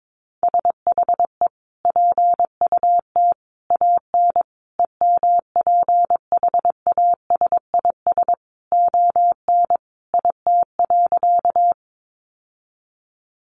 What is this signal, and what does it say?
Morse code, heard as a keyed tone, about 22 wpm, 700 Hz. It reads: SHE PUT AN EMPHASIS ON IT.
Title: Sentences from Top 1400 Words 22wpm